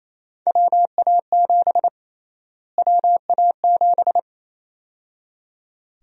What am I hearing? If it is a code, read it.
Morse code, heard as a keyed tone, about 28 wpm, 700 Hz. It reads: WA7 WA7
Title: USA Call Sign Prefixes - Repeated 2x - 28wpm